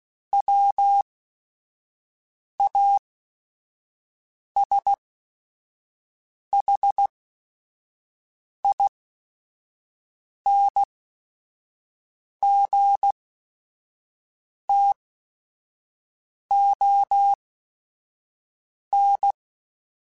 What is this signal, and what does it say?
Morse code, heard as a keyed tone, about 16 wpm, spaced as for 5 wpm, 800 Hz. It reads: WASHINGTON
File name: QSOs / QSOs-mp3s / WD0BGP 1@5